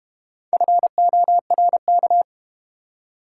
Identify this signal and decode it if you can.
Morse code, heard as a keyed tone, about 30 wpm, 700 Hz. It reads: FORK
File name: Easy Words - Speed-Racing with Hints - 25wpm